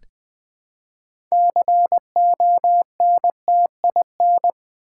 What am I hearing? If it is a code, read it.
Morse code, heard as a keyed tone, about 20 wpm, 700 Hz. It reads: CONTIN
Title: Build A Word - 3 Letters At A Time - 9-Letter Words - 20wpm